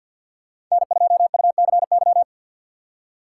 Morse code, heard as a keyed tone, about 52 wpm, 700 Hz. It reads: N2UCY
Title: USA Call Signs - 5 Characters - 2x Character Spacing - 50wpm